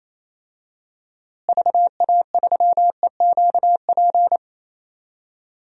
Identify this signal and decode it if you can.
Morse code, heard as a keyed tone, about 28 wpm, 700 Hz. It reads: VA3EQP
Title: DXCC-100 Call Signs - 6 Characters - Repeated 3x - 28wpm